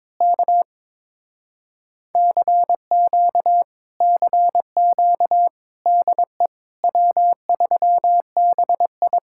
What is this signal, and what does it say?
Morse code, heard as a keyed tone, about 22 wpm, 700 Hz. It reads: K CQ CQ DE W3BI
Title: Speed-Racing Common QSO Phrases - 22wpm